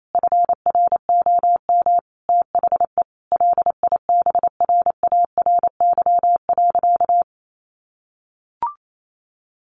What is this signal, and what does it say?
Morse code, heard as a keyed tone, about 28 wpm, 700 Hz. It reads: FROM THE LIBRARY.
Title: Sentences from Top 1300 Words 28wpm